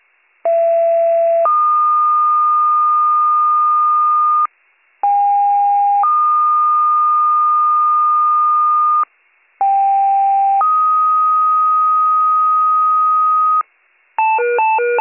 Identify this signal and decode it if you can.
Two-tone page, heard as a one-second tone, then a three-second tone, about 670 then 1220 Hz, about 810 then 1220 Hz, about 790 then 1250 Hz. It is heard from 0.4 to 13.6 s.